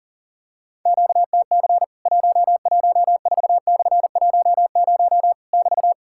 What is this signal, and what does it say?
Morse code, heard as a keyed tone, about 41 wpm, 700 Hz. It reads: QTC 114/10 <BT>
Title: Worked All Europe DX Contest - QTC Exchange - 40wpm